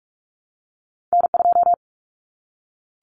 Morse code, heard as a keyed tone, about 46 wpm, 700 Hz. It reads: N2
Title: Common Cut Numbers - 2-Digit 45wpm